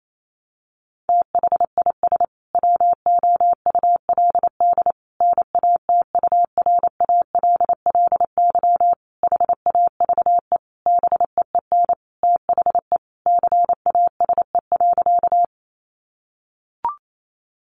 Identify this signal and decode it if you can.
Morse code, heard as a keyed tone, about 28 wpm, 700 Hz. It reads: THIS WOULD NATURALLY HAVE BEEN THE CASE.
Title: Sentences from Top 1300 Words 28wpm